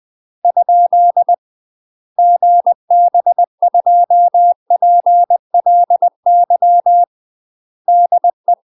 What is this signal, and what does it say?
Morse code, heard as a keyed tone, about 20 wpm, 700 Hz. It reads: ? GB2PLY DE
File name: 20WPM